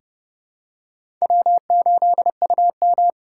Morse code, heard as a keyed tone, about 32 wpm, 700 Hz. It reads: W8UM
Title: USA Call Signs - 4 Characters - 3x Character Spacing - 30wpm